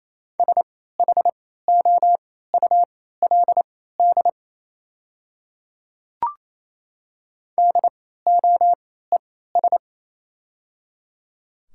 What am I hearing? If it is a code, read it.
Morse code, heard as a keyed tone, about 27 wpm, spaced as for 16 wpm, 700 Hz. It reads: SHOULD DOES
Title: 3x Character Spacing - Top 300 Words - 28wpm